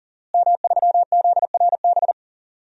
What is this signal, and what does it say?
Morse code, heard as a keyed tone, about 40 wpm, 700 Hz. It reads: M3ZRB
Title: DXCC-100 Call Signs - 5 Characters - Repeated 3x - 40wpm